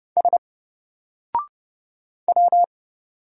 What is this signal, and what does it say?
Morse code, heard as a keyed tone, about 30 wpm, 700 Hz. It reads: S W
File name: Single-Letter-Number-30wpm